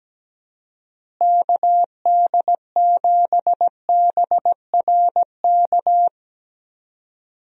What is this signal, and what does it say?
Morse code, heard as a keyed tone, about 17 wpm, 700 Hz. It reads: KD7BRK
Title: USA Call Signs - 6 Characters - Repeated 3x - 17wpm